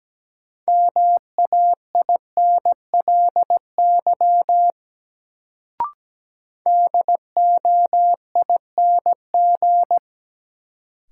Morse code, heard as a keyed tone, about 17 wpm, 700 Hz. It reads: MAINLY DOING
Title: Sentences from Top 1400 Words - New Word Drill Alt - 17wpm